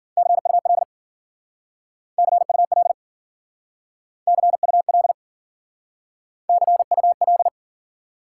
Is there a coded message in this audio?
Yes: Morse code from 0.2 to 7.5 s, heard as a keyed tone, about 52 wpm, 700 Hz. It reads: CUL CUL CUL CUL